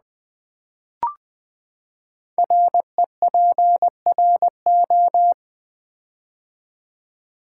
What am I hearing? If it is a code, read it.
Morse code, heard as a keyed tone, about 20 wpm, 700 Hz. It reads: REPRO
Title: Build A Word - 5 Letters At A Time - 12-Letter Words - 20wpm